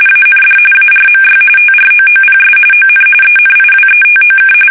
RTTY, about 45.45 baud. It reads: AND ARMY TROOPS ON ENIWETOK
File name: fsk850 / battleshipnj